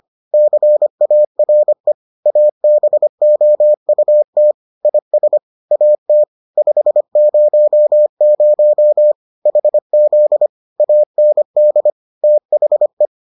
Morse code, heard as a keyed tone, about 25 wpm, 600 Hz. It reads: CARE ABOUT IS AT 500 HZ AND THE